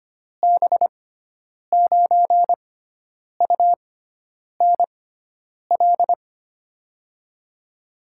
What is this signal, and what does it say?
Morse code, heard as a keyed tone, about 25 wpm, spaced as for 9 wpm, 700 Hz. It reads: B9UNL
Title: USA Call Signs - 6 Characters - 6x Character Spacing - 25wpm